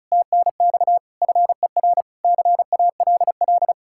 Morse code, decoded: TNX FER CALL

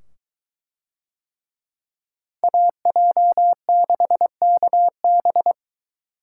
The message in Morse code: AJ6KB

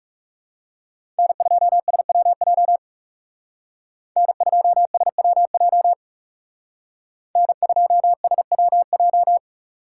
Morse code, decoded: N2SWJ N2SWJ N2SWJ

